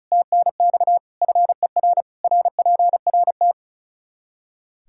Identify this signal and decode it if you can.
Morse code, heard as a keyed tone, about 34 wpm, 700 Hz. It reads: TNX FER RPRT
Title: K1USN SST QSO Practice 35wpm